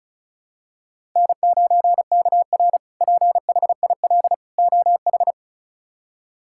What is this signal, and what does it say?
Morse code, heard as a keyed tone, about 34 wpm, 700 Hz. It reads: N9KR PHIL OH
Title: K1USN SST QSO Practice - 35wpm